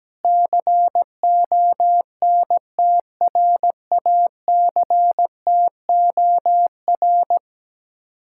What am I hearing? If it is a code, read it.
Morse code, heard as a keyed tone, about 17 wpm, 700 Hz. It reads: CONTRACTOR
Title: Speed-Racing Common QSO Phrases - 17wpm